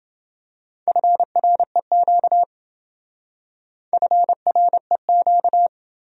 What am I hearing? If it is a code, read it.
Morse code, heard as a keyed tone, about 27 wpm, 700 Hz. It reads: FREQ FREQ